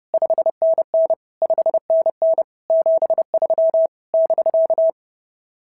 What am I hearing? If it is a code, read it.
Morse code, heard as a keyed tone, about 30 wpm, 650 Hz. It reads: 5NN 5NN 73 <BK>
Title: SOTA QSO Practice 30wpm